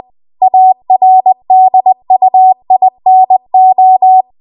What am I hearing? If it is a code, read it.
Morse code, heard as a keyed tone, about 20 wpm, 750 Hz. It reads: ARDUINO